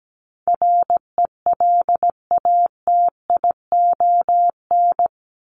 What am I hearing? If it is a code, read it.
Morse code, heard as a keyed tone, about 17 wpm, 700 Hz. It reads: RELATION
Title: ES:S Suffix - Top 100 Words with Base Words 17wpm